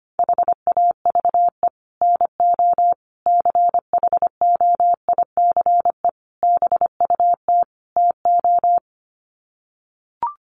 Morse code, heard as a keyed tone, about 25 wpm, 700 Hz. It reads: HAVE NO CHOICE BUT TO
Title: Sets of 5 Words Encore 25wpm